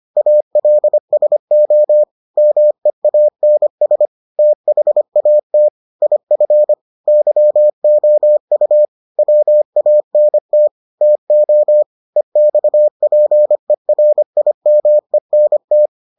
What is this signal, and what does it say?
Morse code, heard as a keyed tone, about 25 wpm, 600 Hz. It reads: ALSO MEANS THAT IF YOU WANT TO EXPERIMENT